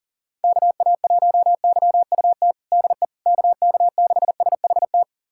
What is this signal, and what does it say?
Morse code, heard as a keyed tone, about 37 wpm, 700 Hz. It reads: KA1YUT DE KK6SHT
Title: USA Callsign DE Callsign 40wpm